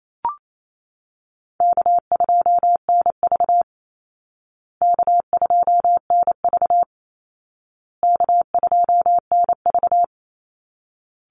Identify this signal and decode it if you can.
Morse code, heard as a keyed tone, about 28 wpm, 700 Hz. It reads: K2NV K2NV K2NV
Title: Call Signs Repeated 2x - 28wpm